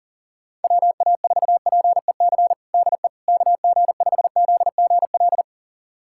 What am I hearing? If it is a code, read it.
Morse code, heard as a keyed tone, about 40 wpm, 700 Hz. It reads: WA4PEC DE KG5ZGL
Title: USA Callsign DE Callsign - 6 Characters - 40wpm